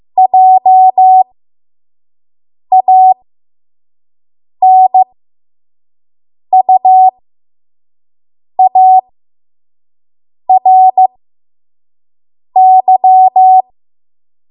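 Morse code, heard as a keyed tone, about 15 wpm, spaced as for 5 wpm, 750 Hz. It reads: JANUARY
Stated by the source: 5 WPM Morse code practice.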